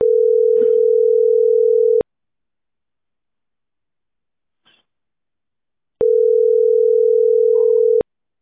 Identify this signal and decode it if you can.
Ringback tone (North America), 440 + 480 Hz.